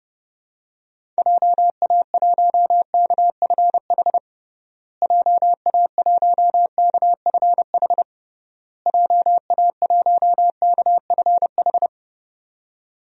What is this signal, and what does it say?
Morse code, heard as a keyed tone, about 32 wpm, 700 Hz. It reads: JA1KFH JA1KFH JA1KFH
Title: DXCC-100 Call Signs - 6 Characters - Repeated 3x - 30wpm